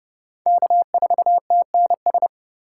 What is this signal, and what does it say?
Morse code, heard as a keyed tone, about 26 wpm, 700 Hz. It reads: K4TNS